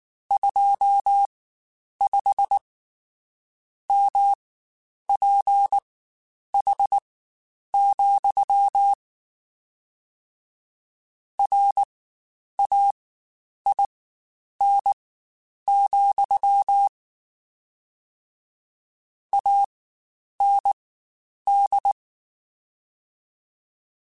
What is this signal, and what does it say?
Morse code, heard as a keyed tone, about 19 wpm, spaced as for 8 wpm, 800 Hz. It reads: 25MPH, RAIN, AND
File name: Spaced Characters / Spaced Characters-mp3s / Weather5 5 sp @18